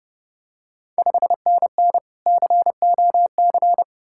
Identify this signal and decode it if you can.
Morse code, heard as a keyed tone, about 31 wpm, 700 Hz. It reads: 5NN COC
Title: Ontario QSO Party Exchange 30wpm